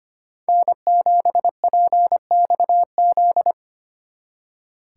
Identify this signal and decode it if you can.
Morse code, heard as a keyed tone, about 29 wpm, 700 Hz. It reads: N7PXZ
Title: Speed Racing - USA Call Signs 25wpm